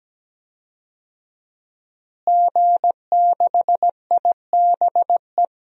Morse code, heard as a keyed tone, about 17 wpm, 700 Hz. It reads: G6IBE
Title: DXCC-100 Call Signs - 5 Characters - Repeated 2x - 17wpm